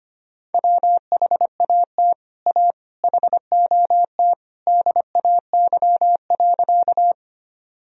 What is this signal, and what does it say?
Morse code, heard as a keyed tone, about 25 wpm, 700 Hz. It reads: WHAT A HOT DAY.
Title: Sentences from Top 1000 Words - Review Part 3 - 25wpm